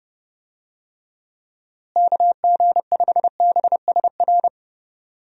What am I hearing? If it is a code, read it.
Morse code, heard as a keyed tone, about 30 wpm, 700 Hz. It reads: KG5BSR
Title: USA Call Signs - 6 Characters - Repeated 3x - 30wpm